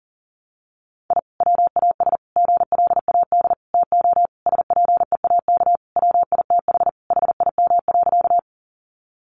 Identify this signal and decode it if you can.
Morse code, heard as a keyed tone, about 40 wpm, 700 Hz. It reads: I WAS GLAD TO SPEAK WITH HIM.
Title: Sentences from Top 1100 Words 40wpm